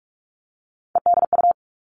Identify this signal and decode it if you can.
Morse code, heard as a keyed tone, about 44 wpm, 700 Hz. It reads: EDU
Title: All Cut Numbers - 3-Digit 45wpm